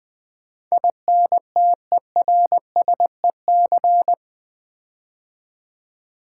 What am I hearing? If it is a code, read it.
Morse code, heard as a keyed tone, about 20 wpm, 700 Hz. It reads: INTERSEC
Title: Build A Word - 4 Letters At A Time - 12-Letter Words - 20wpm